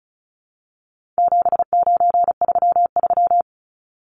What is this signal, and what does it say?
Morse code, heard as a keyed tone, about 34 wpm, 700 Hz. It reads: 7933
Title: Numbers - 4-Digit 35wpm